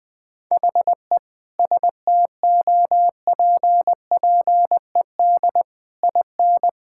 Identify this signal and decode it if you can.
Morse code, heard as a keyed tone, about 20 wpm, 700 Hz. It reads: HE STOPPED IN
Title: Sentences with 3-Word Phrases - 20wpm